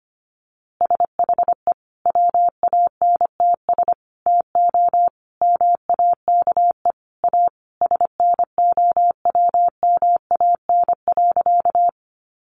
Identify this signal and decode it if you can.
Morse code, heard as a keyed tone, about 25 wpm, 700 Hz. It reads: SHE WANTS TO MAKE A SNOWMAN.
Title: Easy Phrases 25wpm